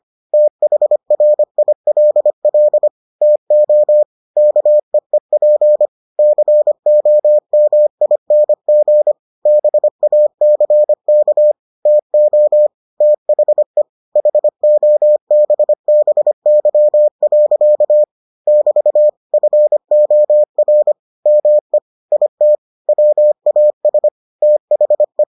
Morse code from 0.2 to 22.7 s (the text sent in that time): THRILL TO KEEP COMING BACK TO THE HOBBY. <BT> FOR ME IT